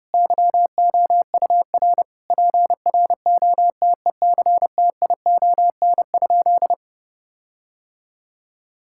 Morse code, heard as a keyed tone, about 30 wpm, 700 Hz. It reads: YOUR PROTECTION?
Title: Sentences from Top 1600 Words 30wpm